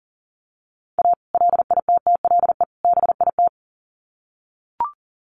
Morse code, heard as a keyed tone, about 40 wpm, 700 Hz. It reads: A LITTLE BIT
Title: Sets of 3 Words 40wpm